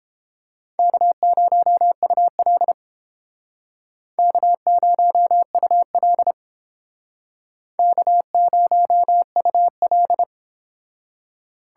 Morse code, heard as a keyed tone, about 30 wpm, 700 Hz. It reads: K0UL K0UL K0UL